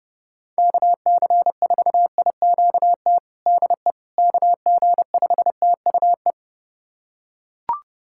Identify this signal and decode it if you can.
Morse code, heard as a keyed tone, about 29 wpm, 700 Hz. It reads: KC4IQT DE KG5TUE
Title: USA Callsign DE Callsign 30wpm